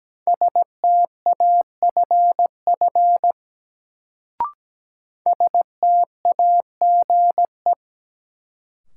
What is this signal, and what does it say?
Morse code, heard as a keyed tone, about 17 wpm, 700 Hz. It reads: STAFF STAGE
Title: Sentences from Top 1200 Words - New Word Drill - 17wpm